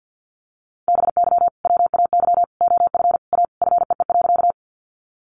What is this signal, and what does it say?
Morse code, heard as a keyed tone, about 48 wpm, 700 Hz. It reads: BY WAY OF A FEE.
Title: Sentences from Top 1500 Words 50wpm